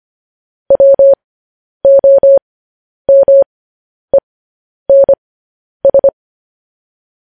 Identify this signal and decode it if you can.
Morse code, heard as a keyed tone, about 25 wpm, spaced as for 10 wpm, 550 Hz. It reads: WOMENS
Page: news-headlines-in-morse-code-at-10-wpm